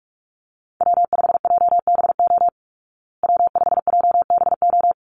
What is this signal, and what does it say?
Morse code, heard as a keyed tone, about 44 wpm, 700 Hz. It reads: W5JBO W5JBO
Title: Call Signs Repeated 3x - 45wpm